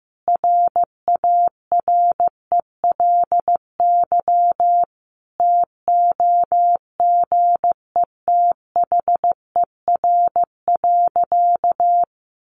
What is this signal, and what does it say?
Morse code, heard as a keyed tone, about 15 wpm, 700 Hz. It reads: RARELY TOGETHER.
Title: Sentences from Top 1400 Words 15wpm